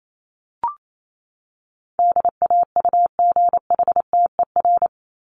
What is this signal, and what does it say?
Morse code, heard as a keyed tone, about 28 wpm, 700 Hz. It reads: DAUGHTER
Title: Top 1251-1500 Words 28wpm